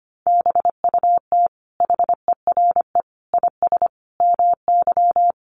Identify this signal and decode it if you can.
Morse code, heard as a keyed tone, about 25 wpm, 700 Hz. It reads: BUT HERE IS MY
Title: Sentences from Top 1500 Words 25wpm